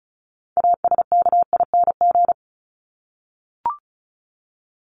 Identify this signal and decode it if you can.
Morse code, heard as a keyed tone, about 34 wpm, 700 Hz. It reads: ASKING
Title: ING Suffix - Top 100 Words with Base Words 35wpm